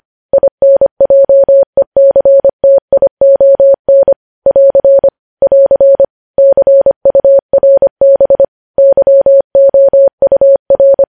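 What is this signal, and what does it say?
Morse code, heard as a keyed tone, about 25 wpm, 550 Hz. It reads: INJECTION <AR> <AR> CURB YOUR
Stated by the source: News headlines sent in Morse at 25 wpm.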